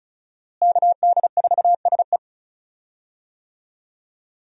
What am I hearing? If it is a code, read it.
Morse code, heard as a keyed tone, about 34 wpm, 700 Hz. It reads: KD4SE
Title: USA Call Signs - 5 Characters - Repeated 2x - 35wpm